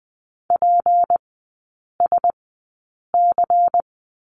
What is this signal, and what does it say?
Morse code, heard as a keyed tone, about 20 wpm, spaced as for 8 wpm, 690 Hz. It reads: PSC
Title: 172 Lesson 28e - Number 9 - Three Characters 20wpm